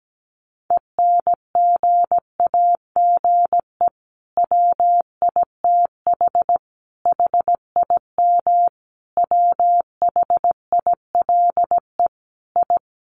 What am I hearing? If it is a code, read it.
Morse code, heard as a keyed tone, about 17 wpm, 700 Hz. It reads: ENGAGE WITH HIM WHILE I
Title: Sentences from Top 1500 Words 17wpm